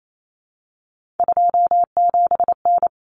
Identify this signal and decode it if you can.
Morse code, heard as a keyed tone, about 28 wpm, 700 Hz. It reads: 27N
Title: Common Cut Numbers - 3-Digit 28wpm